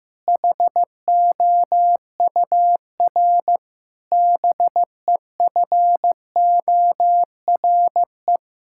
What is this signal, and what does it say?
Morse code, heard as a keyed tone, about 15 wpm, 700 Hz. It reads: HOUR BEFORE
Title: Sentences from Top 1000 Words - Review Part 6 - 15wpm